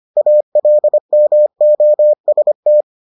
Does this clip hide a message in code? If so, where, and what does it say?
Yes: Morse code from 0.2 to 2.8 s, heard as a keyed tone, about 25 wpm, 600 Hz. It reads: ALMOST